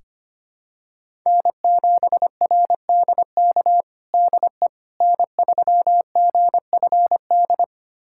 Morse code, heard as a keyed tone, about 25 wpm, 700 Hz. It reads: N7RDK DE N3GFD
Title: USA Callsign DE Callsign - 5 Characters - 25wpm